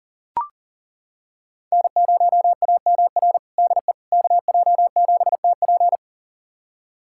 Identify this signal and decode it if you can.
Morse code, heard as a keyed tone, about 40 wpm, 690 Hz. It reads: N0AMR DE KJ7TP
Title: USA Callsign DE Callsign - 5 Characters - 40wpm